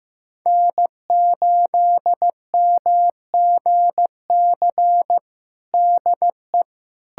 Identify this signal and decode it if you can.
Morse code, heard as a keyed tone, about 15 wpm, 700 Hz. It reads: N8MGC DE
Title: USA Callsign DE Callsign 15wpm